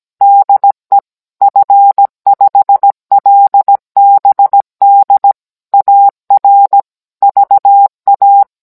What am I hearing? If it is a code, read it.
Morse code, heard as a keyed tone, about 17 wpm, 800 Hz. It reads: DE F5LBD AR VA